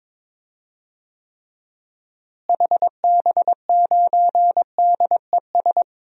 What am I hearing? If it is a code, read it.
Morse code, heard as a keyed tone, about 22 wpm, 700 Hz. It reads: HB9DES
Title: DXCC-100 Call Signs - 6 Characters - Repeated 2x - 22wpm